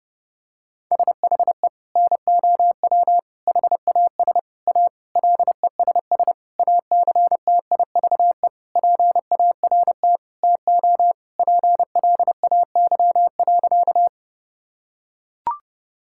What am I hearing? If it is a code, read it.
Morse code, heard as a keyed tone, about 30 wpm, 700 Hz. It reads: SHE NOW HAS A LESS ACTIVE PART TO PLAY.
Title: Sentences from Top 1000 Words - Review Part 6 - 30wpm